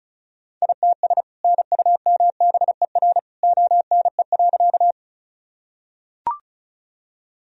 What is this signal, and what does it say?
Morse code, heard as a keyed tone, about 36 wpm, 700 Hz. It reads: ITS NUMBER ONE.
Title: Sentences from Top 1000 Words - Review Part 1 - 35wpm